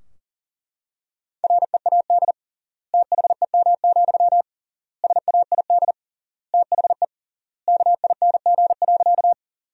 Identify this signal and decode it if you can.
Morse code, heard as a keyed tone, about 40 wpm, 700 Hz. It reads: READ THEM, SAID THE KING.
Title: 3x Word-Spacing - Sentences from Top 400 Words - 40wpm